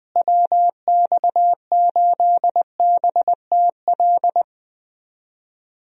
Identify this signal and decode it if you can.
Morse code, heard as a keyed tone, about 20 wpm, 700 Hz. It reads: WX8BTL
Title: USA Call Signs - 6 Characters - Repeated 3x - 20wpm